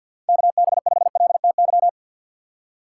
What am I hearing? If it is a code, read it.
Morse code, heard as a keyed tone, about 49 wpm, 700 Hz. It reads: KB5LTY
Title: USA Call Signs - 6 Characters - 3x Character Spacing - 50wpm